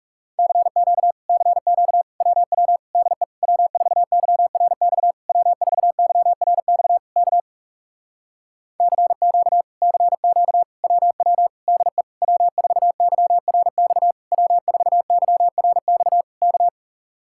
Morse code, heard as a keyed tone, about 37 wpm, 700 Hz. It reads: CQ CQ WW DE W4YRX W4YRX K CQ CQ WW DE W4YRX W4YRX K
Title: Speed-Racing Common QSO Phrases Encore - 30wpm